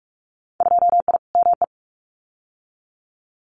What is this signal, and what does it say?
Morse code, heard as a keyed tone, about 46 wpm, 690 Hz. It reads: 2I ME
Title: Winter Field Day Exchanges Encore 45wpm